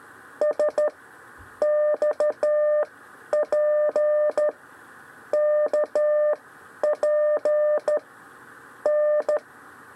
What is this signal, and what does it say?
Morse code, heard as a keyed tone, about 13 wpm, 600 Hz. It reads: S X P K P N
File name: 6 lezione cw 1